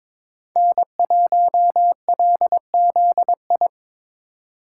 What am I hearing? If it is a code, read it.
Morse code, heard as a keyed tone, about 22 wpm, 700 Hz. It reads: N1LZI
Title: USA Call Signs - 2x Character Spacing - 22wpm